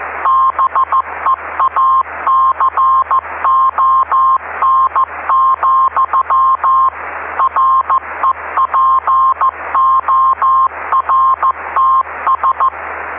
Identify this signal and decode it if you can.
Morse code, heard as a keyed tone, about 15 wpm, 1060 Hz, with noise. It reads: BEACON, REPORTS
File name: ve3we-cw-beacon-28265